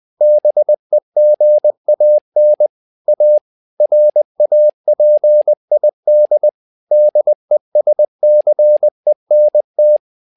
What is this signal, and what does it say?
Morse code, heard as a keyed tone, about 20 wpm, 600 Hz. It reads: BEGAN A RAPID DESCENT
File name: tower trouble 20wpm